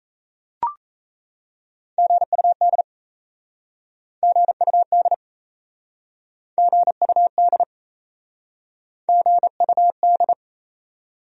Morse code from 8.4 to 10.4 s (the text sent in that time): GUD